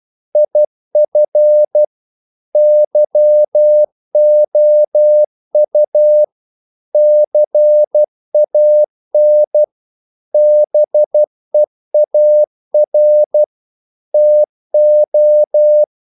Morse code, heard as a keyed tone, about 12 wpm, 600 Hz. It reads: IF YOU CAN BEAR TO
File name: CW / Practice / If12